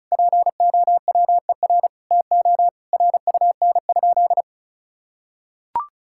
Morse code, heard as a keyed tone, about 36 wpm, 700 Hz. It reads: POWER TO RUN?
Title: Sentences from Top 1000 Words - Review Part 6 - 35wpm